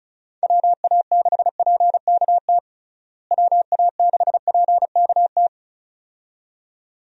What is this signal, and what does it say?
Morse code, heard as a keyed tone, about 34 wpm, 700 Hz. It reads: WA6PKT WA6PKT
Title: USA Call Signs - 6 Characters - Repeated 2x - 35wpm